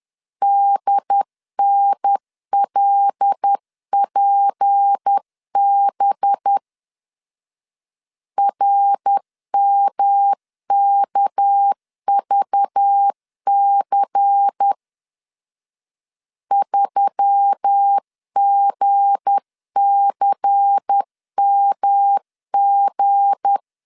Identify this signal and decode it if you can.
Morse code, heard as a keyed tone, about 10 wpm, 800 Hz. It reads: DNLPB RMKVC 3GCMG
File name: LEZIONI / 7 GF9Z / 7 G 10wpm 2501